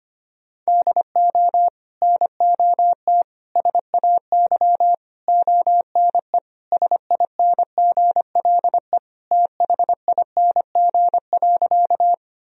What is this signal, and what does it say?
Morse code, heard as a keyed tone, about 25 wpm, 700 Hz. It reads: DO NOT SAY ONE SINGLE THING.
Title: Sentences from Top 1000 Words - Review Part 4 - 25wpm